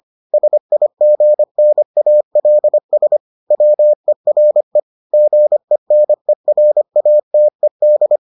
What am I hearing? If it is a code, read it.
Morse code, heard as a keyed tone, about 25 wpm, 600 Hz. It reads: SIGNALS WERE GENERATED